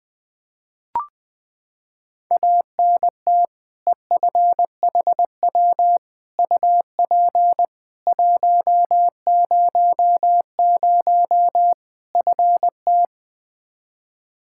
Morse code, heard as a keyed tone, about 20 wpm, 700 Hz. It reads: ANT EFHW UP 100 FT
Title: Practice QSO - Exchange 2 - Info - 20wpm